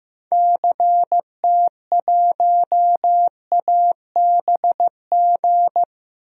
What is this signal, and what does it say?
Morse code, heard as a keyed tone, about 15 wpm, 700 Hz. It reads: CT1ABG
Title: DXCC-100 Call Signs - 6 Characters - Repeated 2x - 15wpm